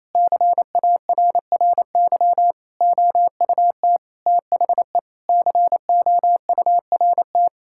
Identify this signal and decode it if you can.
Morse code, heard as a keyed tone, about 28 wpm, 700 Hz. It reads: CARRY OUT THE COURT
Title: Sentences from Top 1600 Words 28wpm